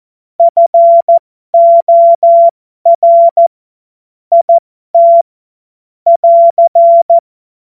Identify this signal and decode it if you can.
Morse code, heard as a keyed tone, about 14 wpm, spaced as for 12 wpm, 680 Hz. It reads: FOR IT <AR>